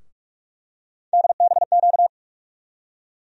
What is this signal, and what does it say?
Morse code, heard as a keyed tone, about 46 wpm, 700 Hz. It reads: DBQ
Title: USA Call Sign Postfixes - 3 Letters 45wpm